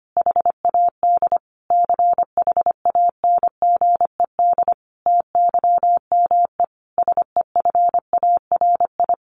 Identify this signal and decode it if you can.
Morse code, heard as a keyed tone, about 25 wpm, 700 Hz. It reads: HAD CHANGED TYME SEFARI
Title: A Sound Of Thunder 25wpm